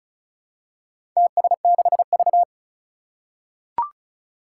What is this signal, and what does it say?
Morse code, heard as a keyed tone, about 36 wpm, 700 Hz. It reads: TS6V
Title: Four Challenging Letter-Number Combinations 35wpm